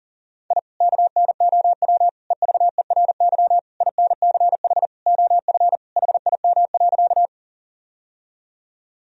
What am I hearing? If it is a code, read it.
Morse code, heard as a keyed tone, about 40 wpm, 700 Hz. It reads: I KNOW EVERY INCH OF HIM.
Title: Sentences from Top 1000 Words - Review Part 3 - 40wpm